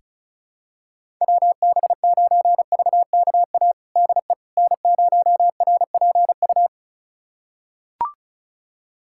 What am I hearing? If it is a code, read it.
Morse code, heard as a keyed tone, about 34 wpm, 700 Hz. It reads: WB9VKA DE N0RPU
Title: USA Callsign DE Callsign 35wpm